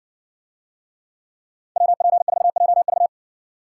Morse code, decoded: WP4JV

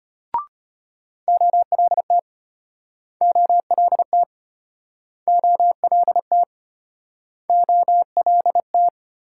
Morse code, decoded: OLT OLT OLT OLT

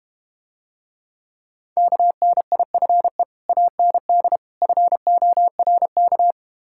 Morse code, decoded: KNIFE AND FORK

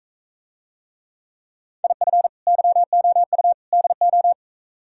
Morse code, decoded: IF YOU DO

